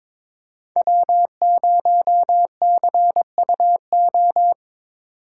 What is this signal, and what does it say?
Morse code, heard as a keyed tone, about 26 wpm, 700 Hz. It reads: W0CUO
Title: Speed Racing - USA 1x3 Call Signs 22wpm